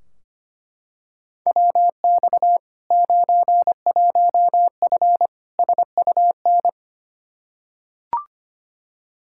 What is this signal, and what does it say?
Morse code, heard as a keyed tone, about 25 wpm, 700 Hz. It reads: WX 91F SUN
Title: Practice QSO - Exchange 2 - Info - 25wpm